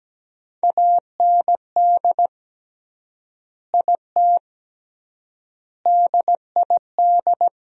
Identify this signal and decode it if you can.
Morse code, heard as a keyed tone, about 17 wpm, 700 Hz. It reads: AND IT DID
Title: 3x Word-Spacing - Sentences from Top 200 Words - 17wpm